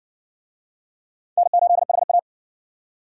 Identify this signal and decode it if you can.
Morse code, heard as a keyed tone, about 57 wpm, 700 Hz. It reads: N8HA